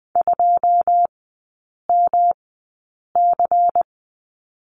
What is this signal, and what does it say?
Morse code, heard as a keyed tone, about 20 wpm, spaced as for 8 wpm, 690 Hz. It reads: 2MC
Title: 145 Lesson 24e - Letter y - Three Characters 20wpm